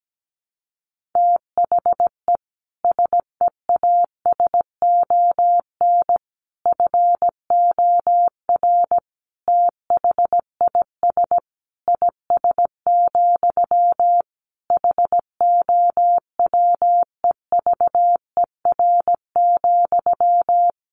Morse code, heard as a keyed tone, about 17 wpm, 700 Hz. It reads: THE SEASON FOR THIS IS, HOWEVER,
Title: Sentences from Top 1400 Words 17wpm